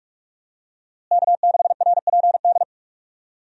Morse code, heard as a keyed tone, about 46 wpm, 700 Hz. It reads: K6RPD
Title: USA Call Signs - 5 Characters - Repeated 2x - 45wpm